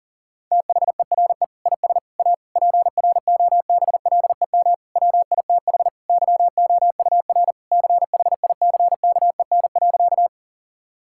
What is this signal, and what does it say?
Morse code, heard as a keyed tone, about 40 wpm, 700 Hz. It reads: THERE IS A PROBLEM WITH YOUR CHICKEN.
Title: Sentences from Top 1000 Words - Review Part 5 - 40wpm